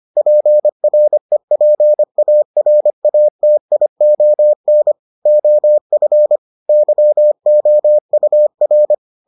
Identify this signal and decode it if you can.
Morse code, heard as a keyed tone, about 25 wpm, 600 Hz. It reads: PREPARATION OF YOUR